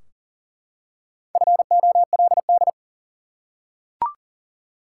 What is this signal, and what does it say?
Morse code, heard as a keyed tone, about 40 wpm, 700 Hz. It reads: FOLD
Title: Four Letter Words - 40wpm